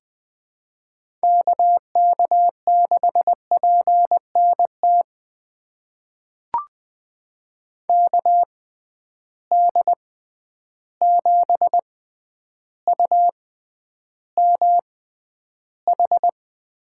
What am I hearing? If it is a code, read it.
Morse code, heard as a keyed tone, about 20 wpm, spaced as for 10 wpm, 700 Hz. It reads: KK6PNT KD7UMH
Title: USA Call Signs - 6 Characters - 6x Character Spacing - 20wpm